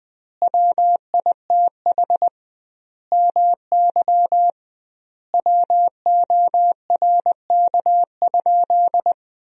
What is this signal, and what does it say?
Morse code, heard as a keyed tone, about 20 wpm, 700 Hz. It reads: WITH MY WORK?
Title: 2x Word-Spacing - Sentences from Top 500 Words - 20wpm